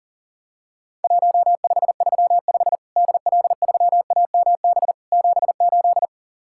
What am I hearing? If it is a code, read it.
Morse code, heard as a keyed tone, about 40 wpm, 700 Hz. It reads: 1535 DL3AMB 78
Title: Worked All Europe DX Contest - QTC Exchange - 40wpm